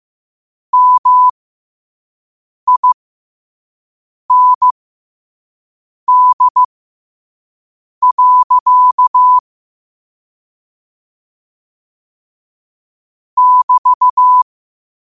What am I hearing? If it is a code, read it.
Morse code, heard as a keyed tone, about 15 wpm, spaced as for 5 wpm, 1000 Hz. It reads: MIND. <BT>
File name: qotdfiles / 5wpm